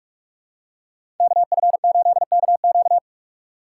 Morse code, heard as a keyed tone, about 46 wpm, 700 Hz. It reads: KF8KQ
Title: USA Call Signs - 5 Characters - 4x Character Spacing - 45wpm